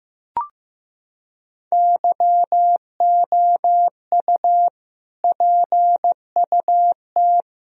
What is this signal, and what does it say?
Morse code, heard as a keyed tone, about 15 wpm, 700 Hz. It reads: YOU PUT
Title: Sentences from Top 1000 Words - Review Part 1 - 15wpm